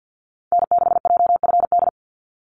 Morse code, heard as a keyed tone, about 50 wpm, 700 Hz. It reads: N6JFD
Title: NA SOTA Activator Callsigns 50wpm